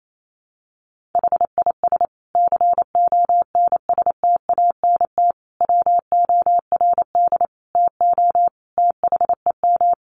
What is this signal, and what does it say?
Morse code, heard as a keyed tone, about 28 wpm, 700 Hz. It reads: HIS CONSTANT WORD TO THEM
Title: Sentences from Top 1500 Words 28wpm